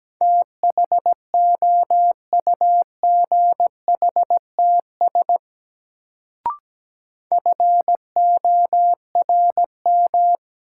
Morse code, heard as a keyed tone, about 17 wpm, 700 Hz. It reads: THOUGHTS FORM
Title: Sentences from Top 1000 Words - Review Part 1 - New Word Drill - 17wpm